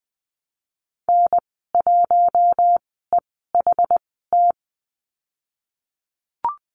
Morse code, heard as a keyed tone, about 19 wpm, spaced as for 14 wpm, 700 Hz. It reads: N1EHT
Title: 088 Lesson 15g - Number 5 - Call Signs 20wpm